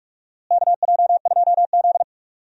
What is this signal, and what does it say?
Morse code, heard as a keyed tone, about 46 wpm, 700 Hz. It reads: KJ2Z
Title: USA Call Signs - 4 Characters - Repeated 2x - 45wpm